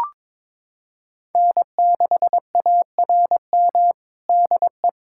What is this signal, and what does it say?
Morse code, heard as a keyed tone, about 21 wpm, 700 Hz. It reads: N6ARM DE
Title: USA Callsign DE Callsign - 5 Characters - 22wpm